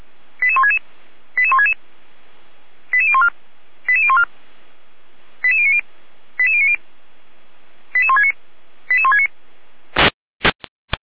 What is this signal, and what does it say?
Five-tone selective call (ZVEI), heard as quick runs of five tones, tones of about 70 ms. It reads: 88166 88166 88014 88014 88090 88090 88169 88169